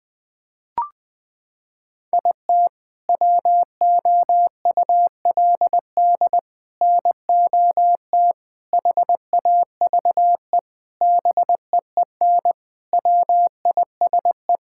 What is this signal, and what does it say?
Morse code, heard as a keyed tone, about 20 wpm, 700 Hz. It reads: IT WOULD NOT HAVE BEEN WISE
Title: Sentences from Top 1600 Words 20wpm